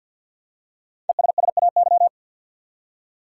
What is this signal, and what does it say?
Morse code, heard as a keyed tone, about 48 wpm, 700 Hz. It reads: ESSAY
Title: Five Letter Words - 50wpm